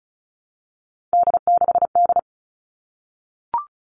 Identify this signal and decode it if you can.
Morse code, heard as a keyed tone, about 34 wpm, 700 Hz. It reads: D6D
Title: All Cut Numbers - 3-Digit 35wpm